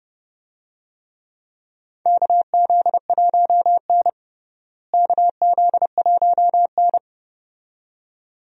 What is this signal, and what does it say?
Morse code, heard as a keyed tone, about 30 wpm, 700 Hz. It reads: KZ1N KZ1N
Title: USA Call Signs - 4 Characters - Repeated 2x - 30wpm